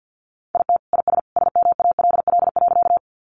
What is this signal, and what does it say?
Morse code, heard as a keyed tone, about 48 wpm, 700 Hz. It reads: IT IS SMALL.
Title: Easy Phrases 50wpm